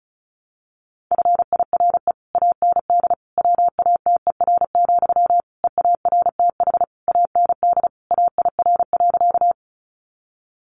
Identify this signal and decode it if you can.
Morse code, heard as a keyed tone, about 34 wpm, 700 Hz. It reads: FIRE AND WATER, EARTH AND AIR.
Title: Sentences from Top 400 Words 35wpm